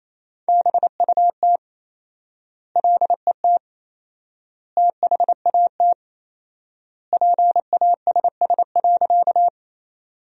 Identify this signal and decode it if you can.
Morse code, heard as a keyed tone, about 28 wpm, 700 Hz. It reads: BUT LET THAT PASS.
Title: 4x Word-Spacing - Sentences from Top 400 Words - 28wpm